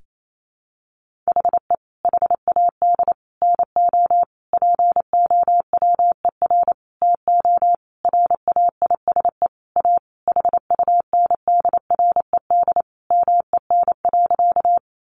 Morse code, heard as a keyed tone, about 28 wpm, 700 Hz. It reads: HE HAD NO POWER TO RAISE A HUNDRED MEN.
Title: Sentences from Top 1400 Words 28wpm